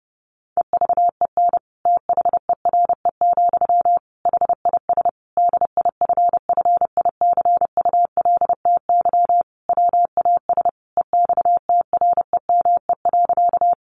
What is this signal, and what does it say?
Morse code, heard as a keyed tone, about 30 wpm, 700 Hz. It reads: EVEN THERE, HIS DIFFICULTY WAS EXTREME.
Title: Sentences from Top 1500 Words 30wpm